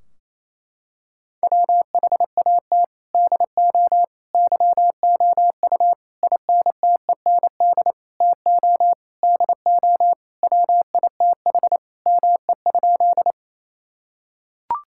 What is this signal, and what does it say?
Morse code, heard as a keyed tone, about 28 wpm, 700 Hz. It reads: WHAT DO YOU INTEND TO DO WITH ME?
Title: Sentences from Top 1600 Words 28wpm